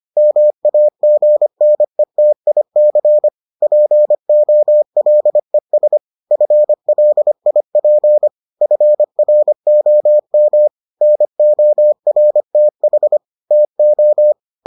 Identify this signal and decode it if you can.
Morse code, heard as a keyed tone, about 25 wpm, 600 Hz. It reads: MAGNETIC POLES FLIP FROM NORTH TO